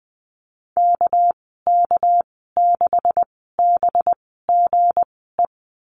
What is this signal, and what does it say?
Morse code, heard as a keyed tone, about 19 wpm, spaced as for 14 wpm, 700 Hz. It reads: KK6BGE
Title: 200 Lesson 32g - Letter k - Call Signs 20wpm